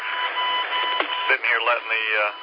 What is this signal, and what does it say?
Speech. There is no code.